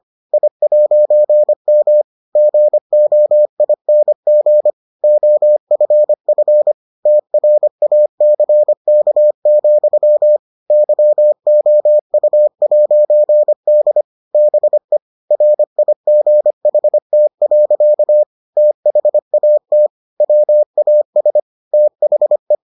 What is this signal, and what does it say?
Morse code, heard as a keyed tone, about 25 wpm, 600 Hz. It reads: I'M GOING OFF TRACK, YOU'D BE RIGHT. THAT WAS THE